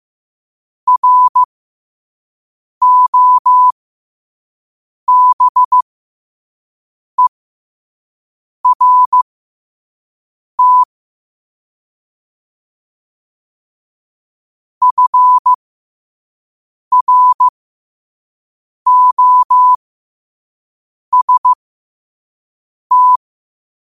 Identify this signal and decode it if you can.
Morse code, heard as a keyed tone, about 15 wpm, spaced as for 5 wpm, 1000 Hz. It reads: ROBERT FROST
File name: qotdfiles / 5wpm